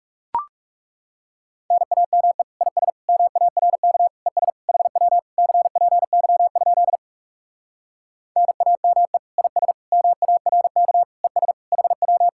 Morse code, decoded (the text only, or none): NAME IS MARK ES HW CPY? NAME IS MARK ES HW